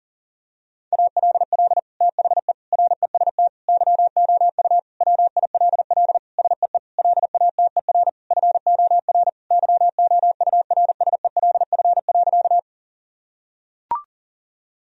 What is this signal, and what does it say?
Morse code, heard as a keyed tone, about 37 wpm, 700 Hz. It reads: ALL THE REST YOU WILL SEE LATER FOR YOURSELF.
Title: Sentences from Top 1000 Words - Review Part 4 - 40wpm